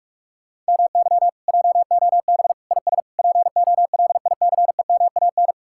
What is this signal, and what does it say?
Morse code, heard as a keyed tone, about 67 wpm, spaced as for 55 wpm, 700 Hz. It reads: MY JOB IS POLICEMAN
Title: Speed-Racing Common QSO Phrases - 45wpm